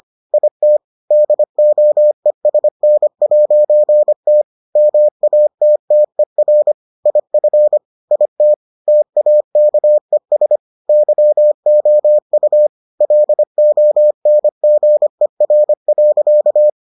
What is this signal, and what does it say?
Morse code, heard as a keyed tone, about 25 wpm, 600 Hz. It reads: IT DOESN'T MATTER IF IT TAKES YOU LONGER.